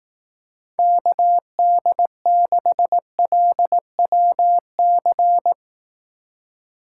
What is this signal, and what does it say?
Morse code, heard as a keyed tone, about 18 wpm, 700 Hz. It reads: KD6LWC
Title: Speed Racing - USA 2x3 Call Signs 15wpm